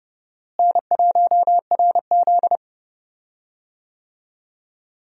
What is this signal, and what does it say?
Morse code, heard as a keyed tone, about 32 wpm, 700 Hz. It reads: N1RZ
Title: USA Call Signs - 4 Characters - Repeated 3x - 30wpm